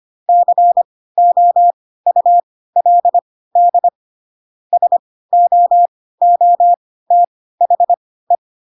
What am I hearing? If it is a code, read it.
Morse code, heard as a keyed tone, about 25 wpm, spaced as for 16 wpm, 700 Hz. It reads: COULD SOOTHE
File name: Pinocchio / Pinocchio ch2 10 25 ik1hge